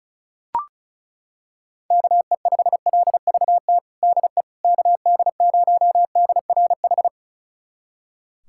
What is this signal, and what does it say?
Morse code, heard as a keyed tone, about 36 wpm, 700 Hz. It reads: KE5LVT DE KD0DRH
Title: USA Callsign DE Callsign - 6 Characters - 35wpm